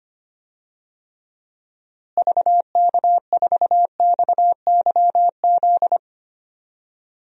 Morse code, heard as a keyed tone, about 25 wpm, 700 Hz. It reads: VK4XYZ
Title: DXCC-100 Call Signs - 6 Characters - Repeated 3x - 25wpm